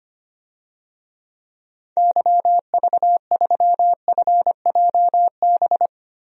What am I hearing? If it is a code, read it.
Morse code, heard as a keyed tone, about 25 wpm, 700 Hz. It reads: YV3FJB
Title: DXCC-100 Call Signs - 6 Characters - 25wpm